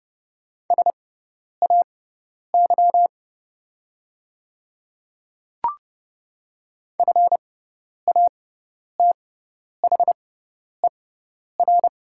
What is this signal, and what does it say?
Morse code, heard as a keyed tone, about 30 wpm, spaced as for 10 wpm, 700 Hz. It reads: SAY FATHER
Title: 6x Character Spacing - Top 200 Words - 30wpm